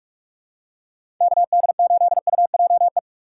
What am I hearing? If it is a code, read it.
Morse code, heard as a keyed tone, about 46 wpm, 700 Hz. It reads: KD8UJE
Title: USA Call Signs - 6 Characters - 6x Character Spacing - 45wpm